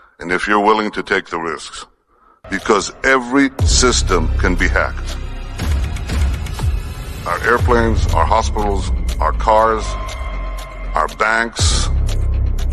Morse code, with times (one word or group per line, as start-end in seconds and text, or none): none